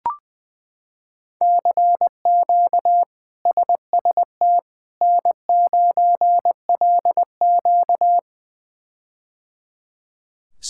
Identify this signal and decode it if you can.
Morse code, heard as a keyed tone, about 21 wpm, 700 Hz. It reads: CQ SST N9LQ
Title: K1USN SST QSO Practice - 20wpm